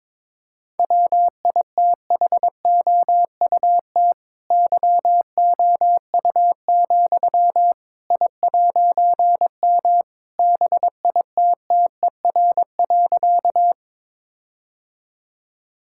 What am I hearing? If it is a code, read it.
Morse code, heard as a keyed tone, about 21 wpm, 700 Hz. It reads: WITHOUT YOU, I'M BITTER.
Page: morse-code-ninja-practice-22wpm